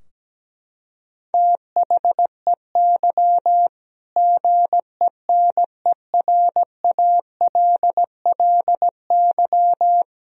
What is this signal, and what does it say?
Morse code, heard as a keyed tone, about 17 wpm, 700 Hz. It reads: THEY GENERALLY
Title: Sentences from Top 1000 Words - Review Part 4 - 17wpm